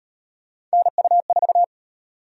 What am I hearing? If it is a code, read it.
Morse code, heard as a keyed tone, about 37 wpm, 700 Hz. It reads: NU4